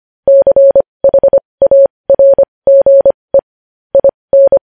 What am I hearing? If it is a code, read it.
Morse code, heard as a keyed tone, about 25 wpm, spaced as for 20 wpm, 550 Hz. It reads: CHARGE IN
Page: News Headlines in Morse Code at 20 WPM